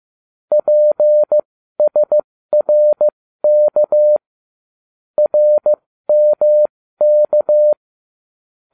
Morse code, heard as a keyed tone, about 15 wpm, spaced as for 13 wpm, 600 Hz. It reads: PSRK RMK